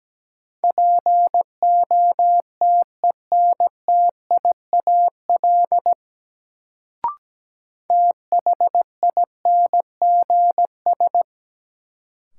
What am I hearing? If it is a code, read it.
Morse code, heard as a keyed tone, about 17 wpm, 700 Hz. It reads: POTENTIAL THINGS
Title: Sentences from Top 1000 Words - Review Part 4 - New Word Drill - 17wpm